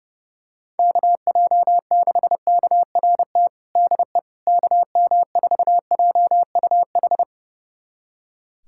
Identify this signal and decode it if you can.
Morse code, heard as a keyed tone, about 30 wpm, 700 Hz. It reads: KJ6KRT DE KM4JUH
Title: USA Callsign DE Callsign 30wpm